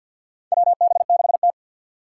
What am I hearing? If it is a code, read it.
Morse code, heard as a keyed tone, about 48 wpm, 700 Hz. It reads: WB6T
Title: USA Call Signs - 4 Characters - 2x Character Spacing - 50wpm